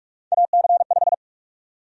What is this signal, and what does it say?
Morse code, heard as a keyed tone, about 42 wpm, 700 Hz. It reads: AC5
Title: USA Call Sign Prefixes - 45wpm